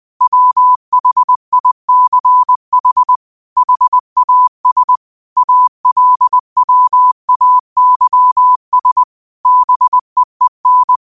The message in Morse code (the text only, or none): WHICH HAS ALWAYS BEEN